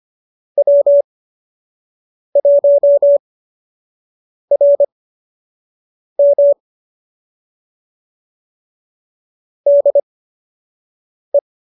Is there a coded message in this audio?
Yes: Morse code from 0.6 to 11.4 s, heard as a keyed tone, about 25 wpm, spaced as for 6 wpm, 570 Hz. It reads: W1RM DE